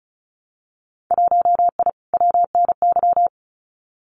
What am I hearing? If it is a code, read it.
Morse code, heard as a keyed tone, about 36 wpm, 700 Hz. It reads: 1I WNY
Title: Winter Field Day Exchanges Encore 35wpm